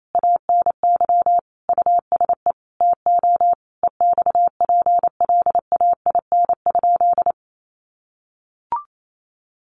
Morse code, heard as a keyed tone, about 28 wpm, 700 Hz. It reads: ANY USE TO EXPLAIN?
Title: Sentences from Top 1100 Words 28wpm